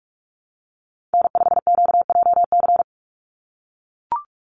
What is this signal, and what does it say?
Morse code, heard as a keyed tone, about 44 wpm, 700 Hz. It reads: N5QJC
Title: USA Call Signs 45wpm